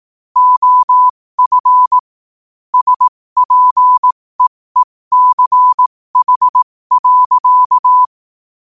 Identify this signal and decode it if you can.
Morse code, heard as a keyed tone, about 18 wpm, spaced as for 15 wpm, 1000 Hz. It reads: OF SPEECH.
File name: qotdfiles / 15wpm